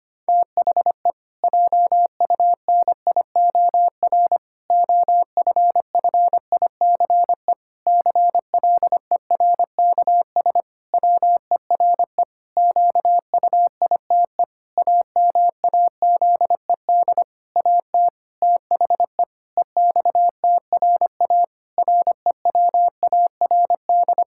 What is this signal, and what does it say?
Morse code, heard as a keyed tone, about 25 wpm, 700 Hz. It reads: THE JUNIOR OFFICE CLERKS WERE QUITE AMAZED AT THE EXTRA REWARD
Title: Pangram Challenge v2 - 25wpm